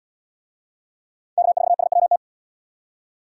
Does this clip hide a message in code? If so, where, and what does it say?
Yes: Morse code from 1.4 to 2.2 s, heard as a keyed tone, about 71 wpm, 700 Hz. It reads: K4SWT